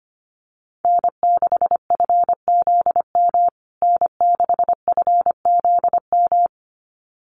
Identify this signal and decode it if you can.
Morse code, heard as a keyed tone, about 25 wpm, 700 Hz. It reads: N6FZM N6FZM
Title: USA Call Signs Repeated 2x 25wpm